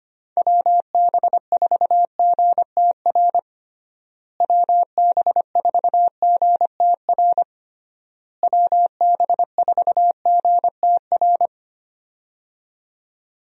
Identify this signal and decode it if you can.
Morse code, heard as a keyed tone, about 25 wpm, 700 Hz. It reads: WB4GTR WB4GTR WB4GTR
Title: USA Call Signs - 6 Characters - Repeated 3x - 25wpm